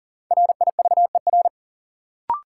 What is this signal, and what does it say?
Morse code, heard as a keyed tone, about 40 wpm, 700 Hz. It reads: RIVER
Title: Sentences from Top 300 Words - New Word Drill - 40wpm